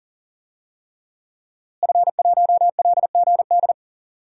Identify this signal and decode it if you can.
Morse code, heard as a keyed tone, about 40 wpm, 700 Hz. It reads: F1LGD
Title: DXCC-100 Call Signs - 5 Characters - Repeated 3x - 40wpm